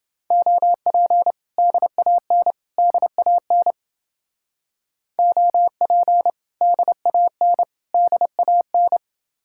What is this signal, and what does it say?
Morse code, heard as a keyed tone, about 31 wpm, 700 Hz. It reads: OP DAN DAN OP DAN DAN